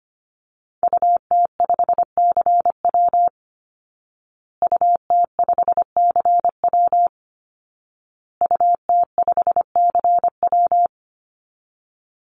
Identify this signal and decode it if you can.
Morse code, heard as a keyed tone, about 25 wpm, 700 Hz. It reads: UT5CW UT5CW UT5CW
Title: Call Signs Repeated 2x - 25wpm